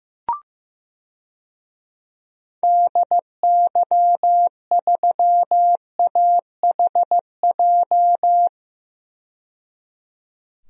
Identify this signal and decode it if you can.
Morse code, heard as a keyed tone, about 15 wpm, 700 Hz. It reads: DY3AHJ
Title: DXCC-100 Call Signs - 6 Characters - 15wpm